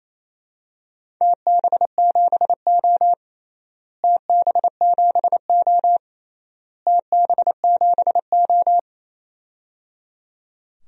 Morse code, heard as a keyed tone, about 28 wpm, 700 Hz. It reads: TB7O TB7O TB7O
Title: DXCC-100 Call Signs - 4 Characters - Repeated 3x - 28wpm